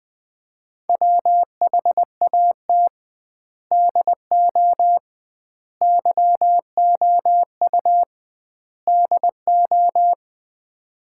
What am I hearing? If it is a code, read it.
Morse code, heard as a keyed tone, about 21 wpm, 700 Hz. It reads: WHAT DO YOU DO
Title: 2x Word-Spacing - Sets of 4 Words - 20wpm